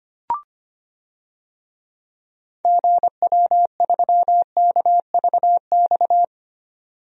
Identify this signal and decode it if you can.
Morse code, heard as a keyed tone, about 25 wpm, 700 Hz. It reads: GW3KVX
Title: DXCC-100 Call Signs - 6 Characters - 25wpm